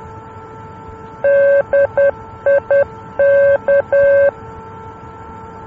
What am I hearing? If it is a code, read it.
Morse code, heard as a keyed tone, about 10 wpm, 560 Hz. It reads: DIK
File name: luxembourg / mx / DIK-L-NDB